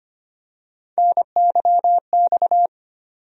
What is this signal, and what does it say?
Morse code, heard as a keyed tone, about 25 wpm, 700 Hz. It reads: NYX
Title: USA Call Sign Postfixes - 3 Letters 25wpm